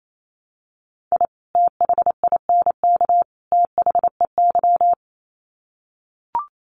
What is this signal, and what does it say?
Morse code, heard as a keyed tone, about 28 wpm, 700 Hz. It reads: I THINK THEY
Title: Sets of 3 Words 28wpm